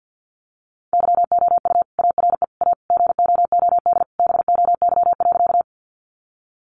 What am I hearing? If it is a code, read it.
Morse code, heard as a keyed tone, about 50 wpm, 700 Hz. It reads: YOU ARE A GOOD BOY.
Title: Sentences from Top 200 Words 50wpm